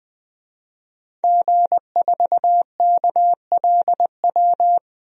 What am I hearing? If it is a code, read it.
Morse code, heard as a keyed tone, about 20 wpm, 700 Hz. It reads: G4KLW
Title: DXCC-100 Call Signs - 5 Characters - Repeated 3x - 20wpm